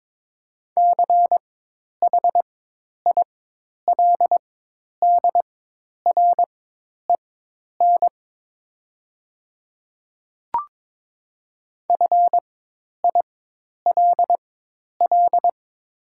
Morse code, heard as a keyed tone, about 22 wpm, spaced as for 10 wpm, 700 Hz. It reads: CHILDREN FILL
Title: 4x Character Spacing - Top 500 Words - 22wpm